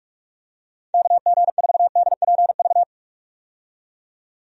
Morse code, heard as a keyed tone, about 46 wpm, 700 Hz. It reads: KG4DPV
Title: USA Call Signs - 6 Characters - 45wpm